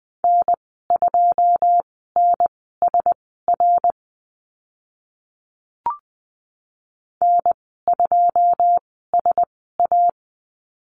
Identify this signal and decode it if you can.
Morse code, heard as a keyed tone, about 20 wpm, spaced as for 15 wpm, 700 Hz. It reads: N2NSR N2SA
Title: 082 Lesson 14g - Number 2 - Call Signs 20wpm